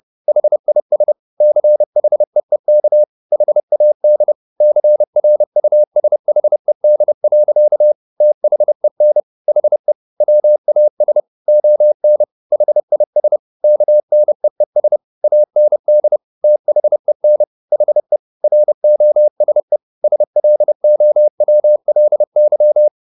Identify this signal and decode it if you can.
Morse code, heard as a keyed tone, about 30 wpm, 600 Hz. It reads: HIS CHEEK HAD CRUSHED. THEN HE WAS ON HIS KNEES AND THEN HE ROSE SLOWLY